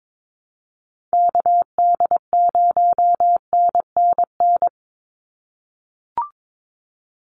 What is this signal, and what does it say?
Morse code, heard as a keyed tone, about 22 wpm, 700 Hz. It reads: KD0NNN
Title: USA Call Signs 22wpm